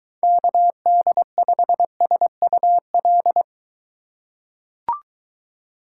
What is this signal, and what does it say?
Morse code, heard as a keyed tone, about 23 wpm, 700 Hz. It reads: KD5SUL